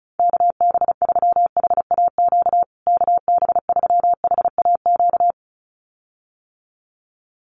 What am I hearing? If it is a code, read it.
Morse code, heard as a keyed tone, about 34 wpm, 700 Hz. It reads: KB3HAQ KB3HAQ
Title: USA Call Signs Repeated 2x 35wpm